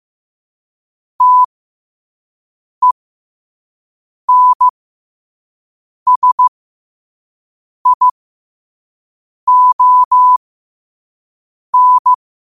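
Morse code, heard as a keyed tone, about 15 wpm, spaced as for 5 wpm, 1000 Hz. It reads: TENSION
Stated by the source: Morse code quote of the day sent at 5 WPM.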